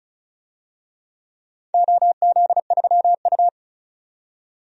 Morse code, heard as a keyed tone, about 34 wpm, 700 Hz. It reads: OZ3U
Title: DXCC-100 Call Signs - 4 Characters - 35wpm